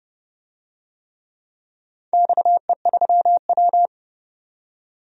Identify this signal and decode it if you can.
Morse code, heard as a keyed tone, about 30 wpm, 700 Hz. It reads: XE3W
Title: DXCC-100 Call Signs - 4 Characters - 30wpm